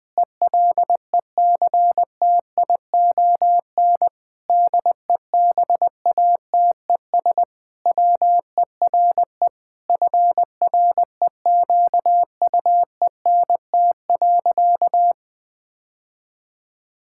Morse code, heard as a keyed tone, about 20 wpm, 700 Hz. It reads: ELECTION DEBATES WERE FREQUENT.
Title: Sentences with 6-Word Phrases - 20wpm